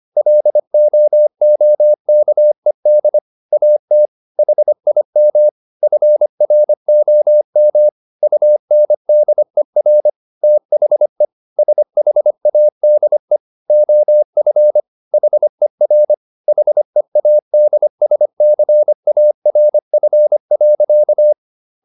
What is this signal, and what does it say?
Morse code, heard as a keyed tone, about 25 wpm, 600 Hz. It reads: LOOKED AT HIM FROM UNDER THE SHADE OF HER HEADSCARF.